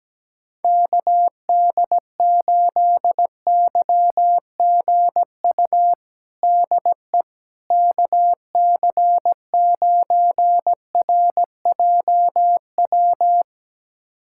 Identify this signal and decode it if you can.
Morse code, heard as a keyed tone, about 17 wpm, 700 Hz. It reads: KD8YGU DE KC9RJW
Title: USA Callsign DE Callsign - 6 Characters - 17wpm